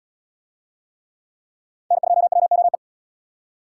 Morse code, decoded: N2WOE

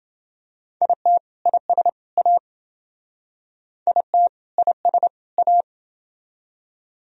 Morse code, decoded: IT IS A IT IS A

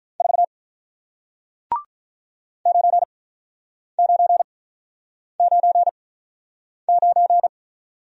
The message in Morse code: 4 9 9 9 9

